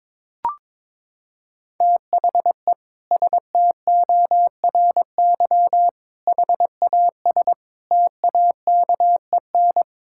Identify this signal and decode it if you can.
Morse code, heard as a keyed tone, about 22 wpm, 700 Hz. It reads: THE STORY HAS TAKEN
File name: practice / 22wpm / Sentences with 8-Word Phrases - 22wpm